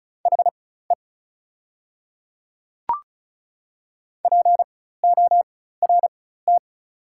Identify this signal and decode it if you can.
Morse code, heard as a keyed tone, about 34 wpm, spaced as for 16 wpm, 690 Hz. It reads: HE PORT
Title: 4x Character Spacing - Top 300 Words - 35wpm